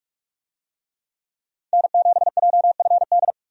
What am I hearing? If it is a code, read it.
Morse code, heard as a keyed tone, about 44 wpm, 700 Hz. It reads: N7JFD
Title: USA Call Signs - 5 Characters - 45wpm